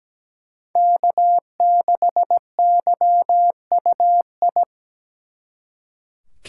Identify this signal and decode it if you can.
Morse code, heard as a keyed tone, about 20 wpm, 700 Hz. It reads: K6YUI
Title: Speed Racing - USA 1x3 Call Signs 17wpm